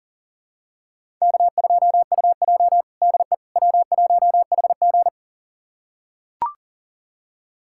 Morse code, heard as a keyed tone, about 40 wpm, 700 Hz. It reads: K2UJ DE W1HG
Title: USA Callsign DE Callsign - 4 Characters - 40wpm